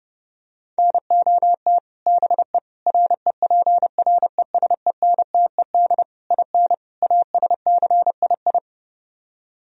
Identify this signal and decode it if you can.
Morse code, heard as a keyed tone, about 30 wpm, 700 Hz. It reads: NOT BE REPRESENTED IN ASCII